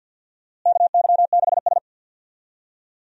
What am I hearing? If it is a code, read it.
Morse code, heard as a keyed tone, about 52 wpm, 700 Hz. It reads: KY6S
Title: USA Call Signs - 4 Characters - Repeated 3x - 50wpm